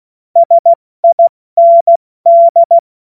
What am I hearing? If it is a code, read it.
Morse code, heard as a keyed tone, about 16 wpm, spaced as for 14 wpm, 680 Hz. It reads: SIND
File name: Witz-07-Oktober-2025-70-F80